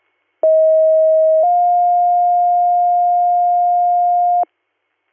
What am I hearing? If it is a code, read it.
Two-tone page, heard as a one-second tone, then a three-second tone, about 640 then 710 Hz. It is heard from 0.4 to 4.4 s.
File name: 200- .595 .0-call 3175